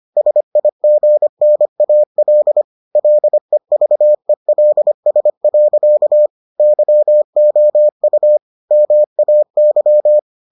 Morse code, heard as a keyed tone, about 25 wpm, 600 Hz. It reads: SIGNAL LEVELS. YOU MAY